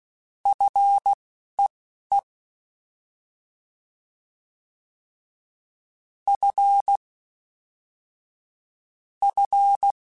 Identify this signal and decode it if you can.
Morse code, heard as a keyed tone, about 16 wpm, spaced as for 11 wpm, 800 Hz. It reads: FEE F F
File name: Fs / Single F mp3s / Single F Prefixes 3x2 Twice @10